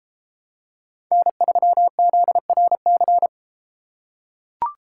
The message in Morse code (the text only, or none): N3ZRC